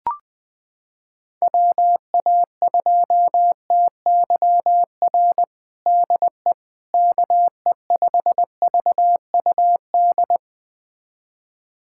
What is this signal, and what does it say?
Morse code, heard as a keyed tone, about 20 wpm, 700 Hz. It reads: WA2TYR DE KE5VUD
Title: USA Callsign DE Callsign - 6 Characters - 20wpm